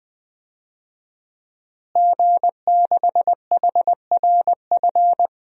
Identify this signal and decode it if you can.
Morse code, heard as a keyed tone, about 20 wpm, 700 Hz. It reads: G6HRF
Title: DXCC-100 Call Signs - 5 Characters - Repeated 2x - 20wpm